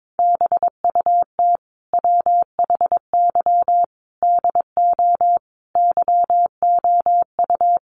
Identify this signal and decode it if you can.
Morse code, heard as a keyed tone, about 22 wpm, 700 Hz. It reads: BUT WHY DO YOU
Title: Sentences from Top 1100 Words 22wpm